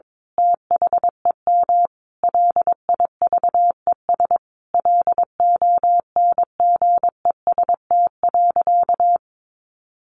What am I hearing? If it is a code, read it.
Morse code, heard as a keyed tone, about 22 wpm, 700 Hz. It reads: THEM LIVES LONGEST.
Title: A Sound Of Thunder 22wpm